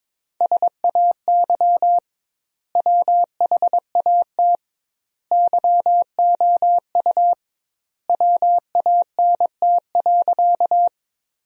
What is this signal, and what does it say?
Morse code, heard as a keyed tone, about 22 wpm, 700 Hz. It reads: SAY WHAT YOU WANT.
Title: 2x Word-Spacing - Sentences from Top 200 Words - 22wpm